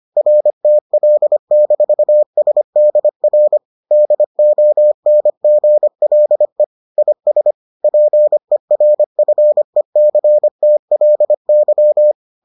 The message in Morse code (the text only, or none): RTL-SDR DONGLE IS PERFECTLY